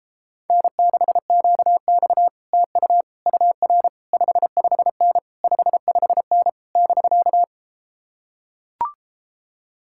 Morse code, heard as a keyed tone, about 33 wpm, 700 Hz. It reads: N6QX TU UR 55N 55N <BK>